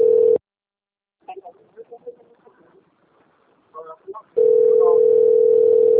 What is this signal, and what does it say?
Ringback tone (North America), 440 + 480 Hz.